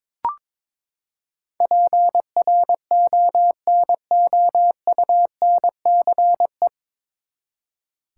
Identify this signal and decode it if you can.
Morse code, heard as a keyed tone, about 22 wpm, 700 Hz. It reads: PRONOUNCE
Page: Morse Code Ninja practice 22wpm